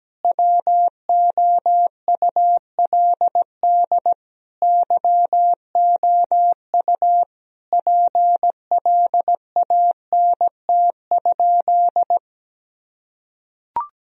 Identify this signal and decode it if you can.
Morse code, heard as a keyed tone, about 17 wpm, 700 Hz. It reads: WOULD YOU PLANT?
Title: Sentences from Top 1000 Words - Review Part 1 - 17wpm